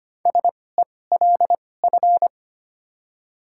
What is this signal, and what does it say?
Morse code, heard as a keyed tone, about 25 wpm, spaced as for 18 wpm, 700 Hz. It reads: SELF
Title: 2x Character Spacing - Top 200 Words - 25wpm